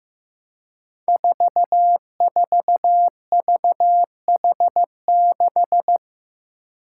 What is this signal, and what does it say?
Morse code, heard as a keyed tone, about 15 wpm, 700 Hz. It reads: 44VH6
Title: Five Challenging Letter-Number Combinations 15wpm